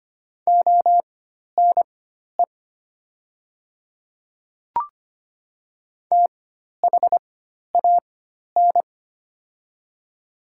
Morse code, heard as a keyed tone, about 25 wpm, spaced as for 12 wpm, 700 Hz. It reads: ONE THAN
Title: 4x Character Spacing - Top 100 Words - 25wpm